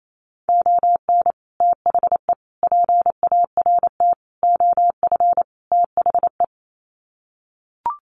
Morse code, heard as a keyed tone, about 28 wpm, 700 Hz. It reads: ON THE PART OF THE
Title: Sets of 5 Words 28wpm